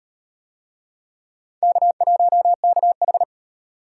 Morse code, heard as a keyed tone, about 38 wpm, 700 Hz. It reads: K1KH